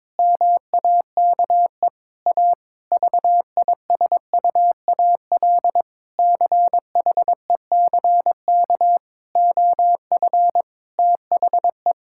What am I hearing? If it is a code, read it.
Morse code, heard as a keyed tone, about 22 wpm, 700 Hz. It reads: MAKE A VISUAL CHECK OF THE
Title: Sentences from Top 1000 Words - Review Part 6 - 22wpm